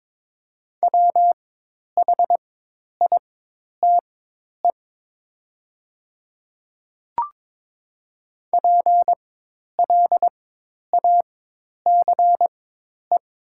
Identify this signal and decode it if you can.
Morse code, heard as a keyed tone, about 22 wpm, spaced as for 10 wpm, 700 Hz. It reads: WHITE PLACE
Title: 4x Character Spacing - Top 300 Words - 22wpm